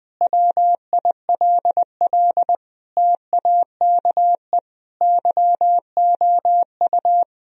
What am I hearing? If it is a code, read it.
Morse code, heard as a keyed tone, about 19 wpm, 700 Hz. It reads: WILL TAKE YOU
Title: Sentences from Top 1000 Words - Review Part 3 - 20wpm